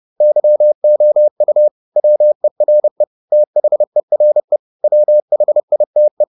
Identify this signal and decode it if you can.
Morse code, heard as a keyed tone, about 30 wpm, 600 Hz. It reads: YOU WERE THERE WHITE